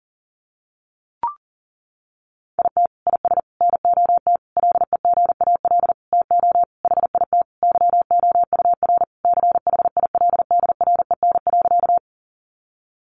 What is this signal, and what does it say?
Morse code, heard as a keyed tone, about 37 wpm, 700 Hz. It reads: IT IS NOT LEGAL TO HIT YOUR CHILDREN.
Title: Sentences from Top 1200 Words 40wpm